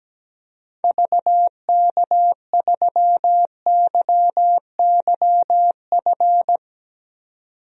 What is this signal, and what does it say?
Morse code, heard as a keyed tone, about 17 wpm, 700 Hz. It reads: VK3YYF
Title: DXCC-100 Call Signs - 6 Characters - 17wpm